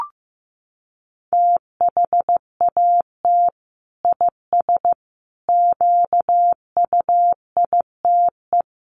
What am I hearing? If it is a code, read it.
Morse code, heard as a keyed tone, about 14 wpm, 700 Hz. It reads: THAT IS QUITE
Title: Sentences from Top 1100 Words 15wpm